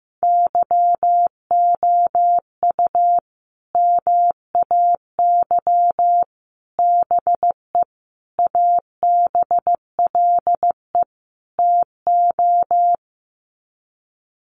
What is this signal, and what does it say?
Morse code, heard as a keyed tone, about 15 wpm, 700 Hz. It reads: YOU MAY BE ABLE TO
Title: Sets of 5 Words Encore 15wpm